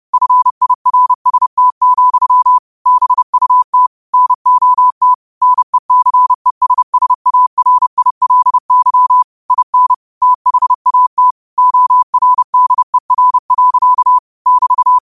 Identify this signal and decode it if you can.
Morse code, heard as a keyed tone, about 30 wpm, 1000 Hz. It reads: FIRST, BUT NOT NECESSARILY IN THAT ORDER. <BT>
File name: qotdfiles / 30wpm